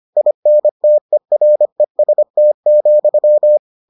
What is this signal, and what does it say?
Morse code, heard as a keyed tone, about 25 wpm, 600 Hz. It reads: INTEREST,